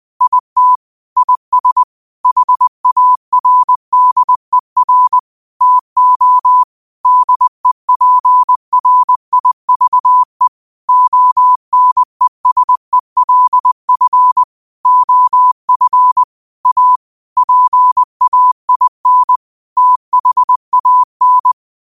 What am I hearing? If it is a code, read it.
Morse code, heard as a keyed tone, about 20 wpm, 1000 Hz. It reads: IT IS HARDER TO DEPRIVE ONESELF OF A PAIN THAN
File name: qotdfiles / 20wpm